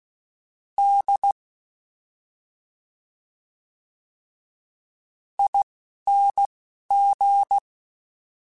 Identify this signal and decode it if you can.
Morse code, heard as a keyed tone, about 16 wpm, spaced as for 5 wpm, 800 Hz. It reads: DING
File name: Special INGs-mp3s / Special INGs-mp3s / Slow ing @10